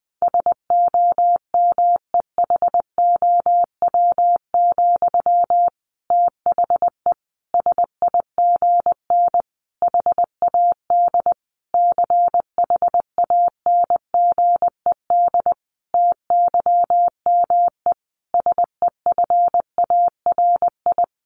Morse code, heard as a keyed tone, about 20 wpm, 700 Hz. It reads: SOMEHOW, THE SIGN HAD CHANGED TYME SEFARI
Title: A Sound Of Thunder 20wpm